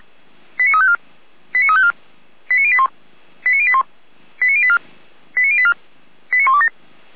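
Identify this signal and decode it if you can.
Five-tone selective call (ZVEI), heard as quick runs of five tones, tones of about 70 ms. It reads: 89354 89354 89081 89081 89084 89084 89127
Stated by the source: ZVEI five-tone selective call 89353 received by SDR.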